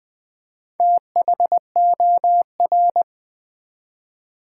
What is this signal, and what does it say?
Morse code, heard as a keyed tone, about 20 wpm, 700 Hz. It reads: THOR
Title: Build A Word - 4 Letters At A Time - 8-Letter Words - 20wpm